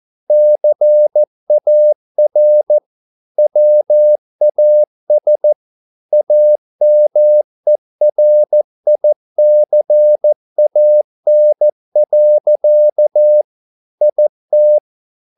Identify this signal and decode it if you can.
Morse code, heard as a keyed tone, about 14 wpm, 600 Hz. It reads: CAR WAS AMERICAN. IT